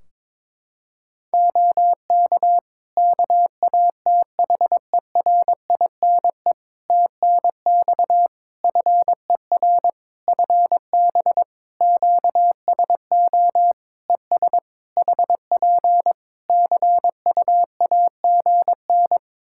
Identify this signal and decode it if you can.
Morse code, heard as a keyed tone, about 22 wpm, 700 Hz. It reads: OK KATHERINE TNX FER FB QSO ES HP CUAGN
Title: K1USN SST QSO Practice 22wpm